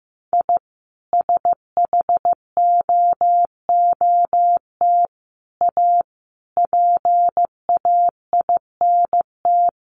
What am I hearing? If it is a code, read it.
Morse code, heard as a keyed tone, about 15 wpm, 700 Hz. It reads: I SHOOT A PAINT
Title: A Sound Of Thunder 15wpm